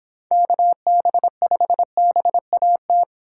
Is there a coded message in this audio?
Yes: Morse code from 0.3 to 3.0 s, heard as a keyed tone, about 26 wpm, 700 Hz. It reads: KB5BAT